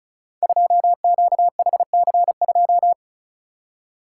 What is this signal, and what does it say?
Morse code, heard as a keyed tone, about 35 wpm, 700 Hz. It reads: 2QHC2
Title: Code Groups - 5 Letter:Numbers - 35wpm